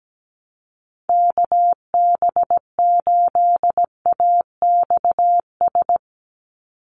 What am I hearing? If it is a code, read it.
Morse code, heard as a keyed tone, about 17 wpm, 700 Hz. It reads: KB8AXS
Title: USA Call Signs 17wpm